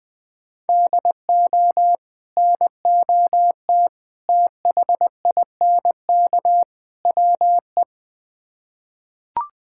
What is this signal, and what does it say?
Morse code, heard as a keyed tone, about 21 wpm, 700 Hz. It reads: DO NOT THINK WE
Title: Build A Phrase - 4-Word Phrases - 20wpm